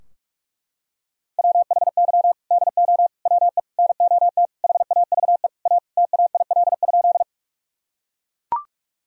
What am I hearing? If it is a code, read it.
Morse code, heard as a keyed tone, about 41 wpm, 700 Hz. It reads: WHY DO WE NOT HAVE A TAIL?
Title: Sentences from Top 1000 Words - Review Part 3 - 45wpm